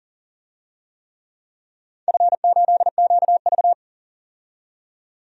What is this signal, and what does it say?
Morse code, heard as a keyed tone, about 40 wpm, 700 Hz. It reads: F8QV
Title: DXCC-100 Call Signs - 4 Characters - 40wpm